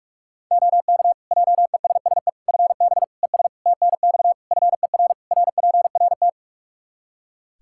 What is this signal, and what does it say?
Morse code, heard as a keyed tone, about 41 wpm, 700 Hz. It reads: OK JESSE FB ES TNX FER RPRT
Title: Practice QSO - Exchange 2 - Info - 45wpm